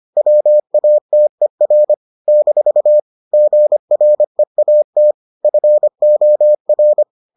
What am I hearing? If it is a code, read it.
Morse code, heard as a keyed tone, about 25 wpm, 600 Hz. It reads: WATER - GREAT FOR